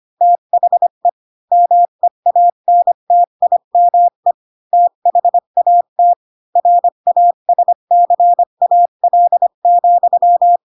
Morse code, heard as a keyed tone, about 25 wpm, 700 Hz. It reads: THE MEANTIME THAT RASCAL,
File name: Pinocchio / Pinocchio ch4 20 25 ik1hge